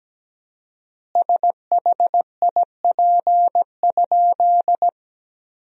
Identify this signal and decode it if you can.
Morse code, heard as a keyed tone, about 17 wpm, 700 Hz. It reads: SHIP?
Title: 4x Word-Spacing - Sentences from Top 400 Words - 17wpm